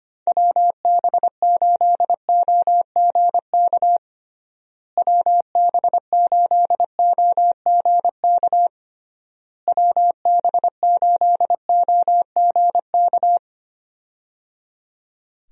Morse code, heard as a keyed tone, about 25 wpm, 700 Hz. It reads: WB8OGK WB8OGK WB8OGK
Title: USA Call Signs - 6 Characters - Repeated 3x - 25wpm